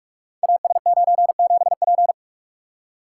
Morse code, heard as a keyed tone, about 44 wpm, 700 Hz. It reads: AS97P
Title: Code Groups - 5 Letter:Numbers - 45wpm